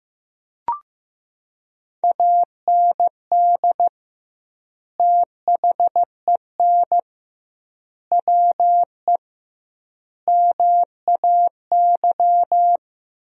Morse code, heard as a keyed tone, about 15 wpm, 700 Hz. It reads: AND THEN WE MAY
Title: 2x Word-Spacing - Sentences from Top 300 Words - 15wpm